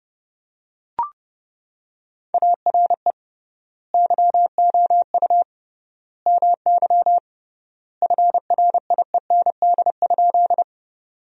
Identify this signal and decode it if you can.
Morse code, heard as a keyed tone, about 28 wpm, 700 Hz. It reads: ARE YOU MY FRIEND?
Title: 3x Word-Spacing - Sentences from Top 300 Words - 30wpm